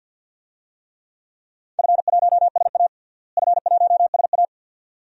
Morse code, heard as a keyed tone, about 50 wpm, 700 Hz. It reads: F1SA F1SA
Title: DXCC-100 Call Signs - 4 Characters - Repeated 2x - 50wpm